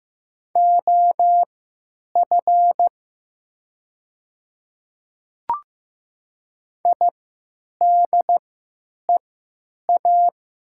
Morse code, heard as a keyed tone, about 15 wpm, spaced as for 8 wpm, 700 Hz. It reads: OF IDEA
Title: 3x Character Spacing - Top 300 Words - 15wpm